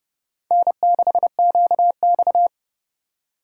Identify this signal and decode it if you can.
Morse code, heard as a keyed tone, about 37 wpm, 700 Hz. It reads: N6QX
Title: Speed Racing - USA 1x2 Call Signs 30wpm